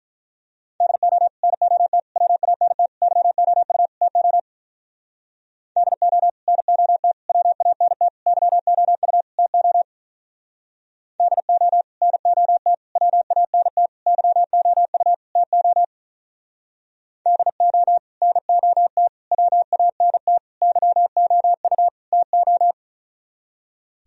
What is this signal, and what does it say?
Morse code, heard as a keyed tone, about 44 wpm, 700 Hz. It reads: DO NOT WANT YOU TO DO NOT WANT YOU TO DO NOT WANT YOU TO DO NOT WANT YOU TO
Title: Speed-Racing - Sets of 5 Words 35wpm